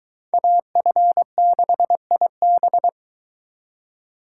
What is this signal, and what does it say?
Morse code, heard as a keyed tone, about 23 wpm, 700 Hz. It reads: AF6IB